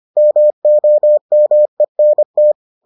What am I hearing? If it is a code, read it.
Morse code, heard as a keyed tone, about 25 wpm, 600 Hz. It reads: MOMENT